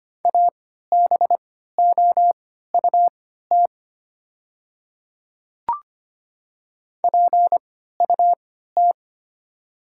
Morse code, heard as a keyed tone, about 25 wpm, spaced as for 14 wpm, 690 Hz. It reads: ABOUT PUT
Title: 3x Character Spacing - Top 200 Words - 25wpm